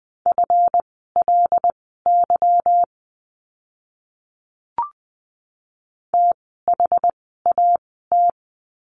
Morse code, heard as a keyed tone, about 20 wpm, spaced as for 15 wpm, 700 Hz. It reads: FLY THAT
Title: 146 Lesson 24f - Letter y - Words 20wpm